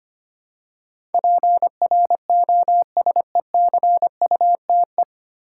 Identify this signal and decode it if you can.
Morse code, heard as a keyed tone, about 25 wpm, 700 Hz. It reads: PROSECUTE
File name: practice / 25wpm / 9 Letter Words - First 3 Letters Given - 25wpm